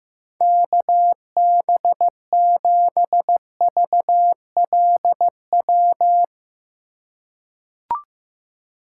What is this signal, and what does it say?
Morse code, heard as a keyed tone, about 15 wpm, 700 Hz. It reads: KB7VLW
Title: USA Call Signs - 6 Characters - 3x Character Spacing - 15wpm